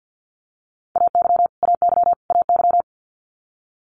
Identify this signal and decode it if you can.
Morse code, heard as a keyed tone, about 50 wpm, 700 Hz. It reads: AY AY AY
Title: Multi-letter Phonemes Repeated 3x 50wpm